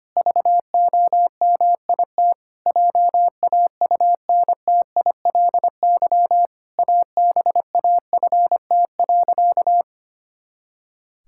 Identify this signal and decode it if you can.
Morse code, heard as a keyed tone, about 25 wpm, 700 Hz. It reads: VOMIT JAUNTILY ABAFT.
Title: Pangram Challenge v2 - 25wpm